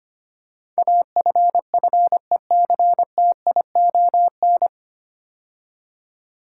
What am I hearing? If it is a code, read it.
Morse code, heard as a keyed tone, about 25 wpm, 700 Hz. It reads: AFFECTION
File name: practice / 25wpm / 9 Letter Words - First 3 Letters Given - 25wpm